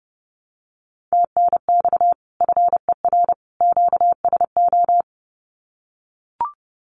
Morse code, heard as a keyed tone, about 30 wpm, 700 Hz. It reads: TNX FER QSO
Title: Common QSO Phrases 30wpm